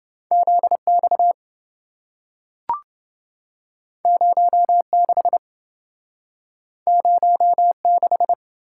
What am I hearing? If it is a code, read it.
Morse code, heard as a keyed tone, about 26 wpm, 700 Hz. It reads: ZX 06 06